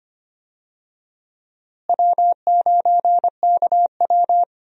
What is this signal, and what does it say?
Morse code, heard as a keyed tone, about 25 wpm, 700 Hz. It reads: W9KW
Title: USA Call Signs - 4 Characters - Repeated 3x - 25wpm